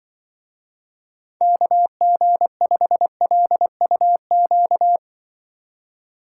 Morse code, heard as a keyed tone, about 23 wpm, 700 Hz. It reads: KG5LUQ
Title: Speed Racing - USA 2x3 Call Signs 20wpm